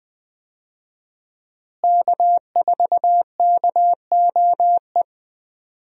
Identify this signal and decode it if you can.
Morse code, heard as a keyed tone, about 20 wpm, 700 Hz. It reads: K4KOE
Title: USA Call Signs - 5 Characters - Repeated 3x - 20wpm